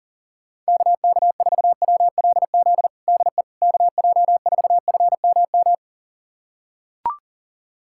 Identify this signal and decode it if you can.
Morse code, heard as a keyed tone, about 40 wpm, 700 Hz. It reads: KK4WLZ DE KJ4FMM
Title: USA Callsign DE Callsign - 6 Characters - 40wpm